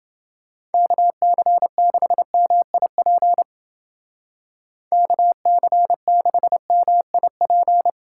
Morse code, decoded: KC6MIP KC6MIP

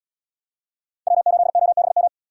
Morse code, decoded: W8OBM

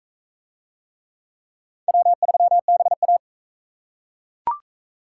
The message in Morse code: W3BA

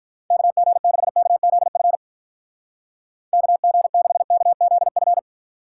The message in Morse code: KG6KZF KG6KZF

